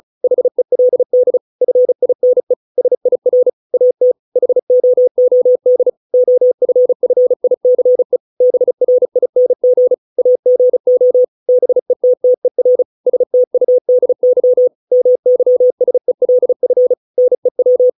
Morse code, heard as a keyed tone, about 34 wpm, 500 Hz. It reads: HELD FINE SIR AT HOOD OFFICE BRING AGO BETTER STUDY MYSELF NEW